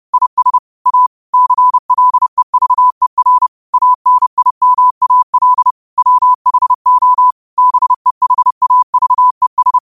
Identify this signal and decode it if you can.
Morse code, heard as a keyed tone, about 30 wpm, 1000 Hz. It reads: IS A CLEVER ANIMAL WHO BEHAVES